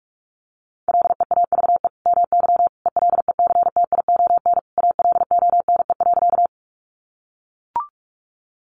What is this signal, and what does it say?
Morse code, heard as a keyed tone, about 44 wpm, 700 Hz. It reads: LEAVE MY ELECTION ALONE.
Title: Sentences from Top 1500 Words 45wpm